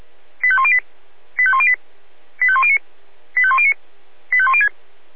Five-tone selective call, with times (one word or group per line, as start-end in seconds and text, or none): 0.4-0.8 85108
1.4-1.8 85108
2.4-2.8 85109
3.4-3.7 85109
4.3-4.7 85106